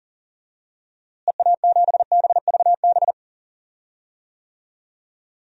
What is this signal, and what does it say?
Morse code, heard as a keyed tone, about 40 wpm, 700 Hz. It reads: EA7BVB
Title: DXCC-100 Call Signs - 6 Characters - Repeated 2x - 40wpm